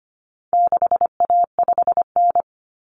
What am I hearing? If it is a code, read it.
Morse code, heard as a keyed tone, about 25 wpm, 700 Hz. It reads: 6A5N
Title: Common Cut Numbers - 4-Digit 25wpm